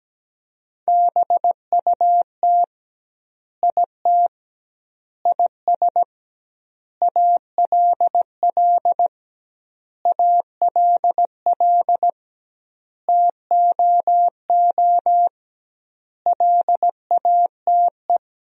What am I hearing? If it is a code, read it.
Morse code, heard as a keyed tone, about 17 wpm, 700 Hz. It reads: BUT IT IS ALL ALL TOO LATE
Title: 2x Word-Spacing - Sentences from Top 300 Words - 17wpm